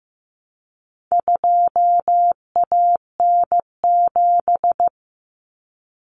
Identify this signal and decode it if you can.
Morse code, heard as a keyed tone, about 15 wpm, 700 Hz. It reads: 2AN7
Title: Common Cut Numbers - 4-Digit 15wpm